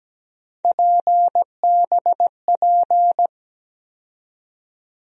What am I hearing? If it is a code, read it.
Morse code, heard as a keyed tone, about 17 wpm, 700 Hz. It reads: PBP
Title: USA Call Sign Postfixes - 3 Letters 17wpm